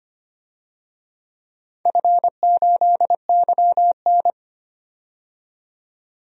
Morse code, heard as a keyed tone, about 25 wpm, 700 Hz. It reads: F8YN
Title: DXCC-100 Call Signs - 4 Characters - 25wpm